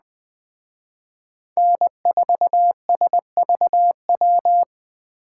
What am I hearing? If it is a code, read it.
Morse code, heard as a keyed tone, about 20 wpm, 700 Hz. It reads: N4SVW
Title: Speed Racing - USA 1x3 Call Signs 17wpm